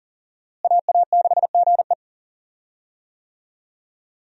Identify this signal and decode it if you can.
Morse code, heard as a keyed tone, about 39 wpm, 700 Hz. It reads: AA6GE
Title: USA Call Signs - 5 Characters - Repeated 3x - 40wpm